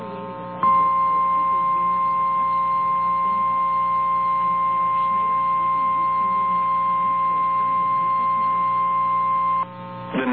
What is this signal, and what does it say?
Emergency Alert System (SAME) tones (the 1050 Hz attention tone). It is heard from 0.6 to 9.7 s.
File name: eas / KNCYKBIE / KNCYKBIE